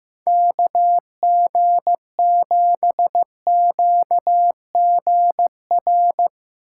Morse code, heard as a keyed tone, about 15 wpm, 700 Hz. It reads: KG7QGR
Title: USA Call Signs - 6 Characters - 3x Character Spacing - 15wpm